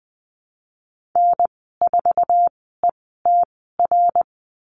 Morse code, heard as a keyed tone, about 20 wpm, spaced as for 14 wpm, 700 Hz. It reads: N4ETR
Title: 088 Lesson 15g - Number 5 - Call Signs 20wpm